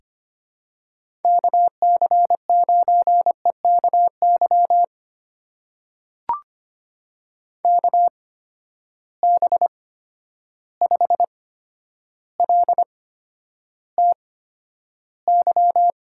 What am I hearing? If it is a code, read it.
Morse code, heard as a keyed tone, about 25 wpm, spaced as for 11 wpm, 700 Hz. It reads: KC9EKY KB5LTY
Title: USA Call Signs - 6 Characters - 8x Character Spacing - 25wpm